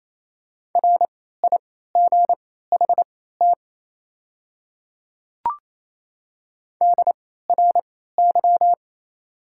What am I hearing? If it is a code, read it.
Morse code, heard as a keyed tone, about 27 wpm, spaced as for 16 wpm, 690 Hz. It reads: RIGHT DRY
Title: 3x Character Spacing - Top 500 Words - 28wpm